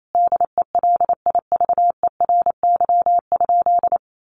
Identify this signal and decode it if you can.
Morse code, heard as a keyed tone, about 28 wpm, 700 Hz. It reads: DELIVERY?
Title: Sentences from Top 1500 Words 28wpm